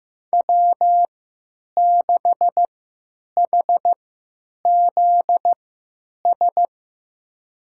Morse code, heard as a keyed tone, about 15 wpm, spaced as for 9 wpm, 700 Hz. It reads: W6HZS
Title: USA Call Signs - 5 Characters - 3x Character Spacing - 15wpm